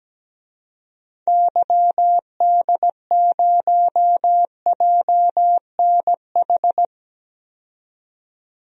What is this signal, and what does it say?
Morse code, heard as a keyed tone, about 17 wpm, 700 Hz. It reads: YD0JNH
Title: DXCC-100 Call Signs - 6 Characters - 17wpm